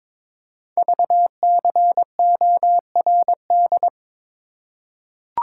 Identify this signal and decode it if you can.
Morse code, heard as a keyed tone, about 23 wpm, 700 Hz. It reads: VCORD
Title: Code Groups - 5 Letters - 22wpm